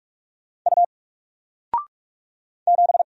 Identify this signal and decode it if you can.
Morse code, heard as a keyed tone, about 44 wpm, 700 Hz. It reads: U 7
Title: Single Letter-Number 45wpm